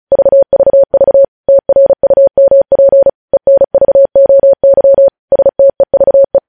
Morse code, heard as a keyed tone, about 36 wpm, 550 Hz. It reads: VVV TRUMP ENVOY STEVE
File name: podcast / News headlines-1x-35x35